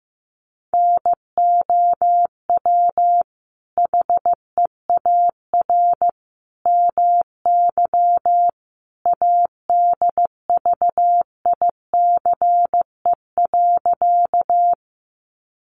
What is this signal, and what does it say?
Morse code, heard as a keyed tone, about 15 wpm, 700 Hz. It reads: NOW HEAR MY ADVICE.
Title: Sentences from Top 1100 Words 15wpm